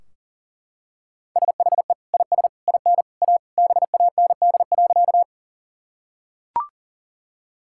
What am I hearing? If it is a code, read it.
Morse code, heard as a keyed tone, about 40 wpm, 700 Hz. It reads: SHE IS IN A BAND.
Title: Sentences from Top 1600 Words 40wpm